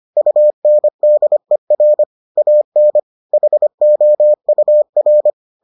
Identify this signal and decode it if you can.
Morse code, heard as a keyed tone, about 25 wpm, 600 Hz. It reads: UNDER AN HOUR